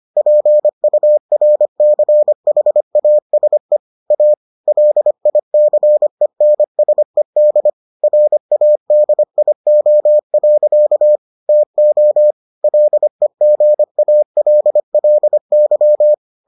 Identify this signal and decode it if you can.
Morse code, heard as a keyed tone, about 25 wpm, 600 Hz. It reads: PURCHASE A LICENSED RADIO. TO LEGALLY